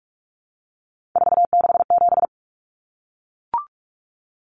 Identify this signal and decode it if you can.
Morse code, heard as a keyed tone, about 44 wpm, 700 Hz. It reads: 467
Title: Numbers - 3-Digit 45wpm